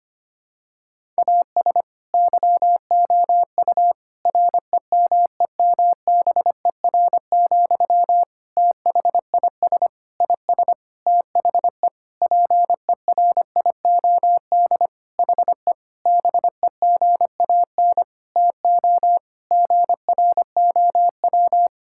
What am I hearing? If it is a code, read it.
Morse code, heard as a keyed tone, about 25 wpm, 700 Hz. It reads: AS YOU REMEMBER, THIS IS THE PERIOD HE BEGAN TO GROW
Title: Sentences from Top 1000 Words - Review Part 5 - 25wpm